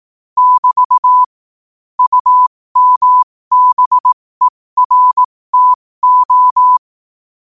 Morse code, heard as a keyed tone, about 18 wpm, spaced as for 15 wpm, 1000 Hz. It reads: <BT> UMBERTO
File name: qotdfiles / 15wpm